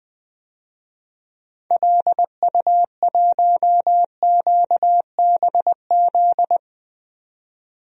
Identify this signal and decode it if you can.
Morse code, heard as a keyed tone, about 20 wpm, 700 Hz. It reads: LU1QBZ
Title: DXCC-100 Call Signs - 6 Characters - 20wpm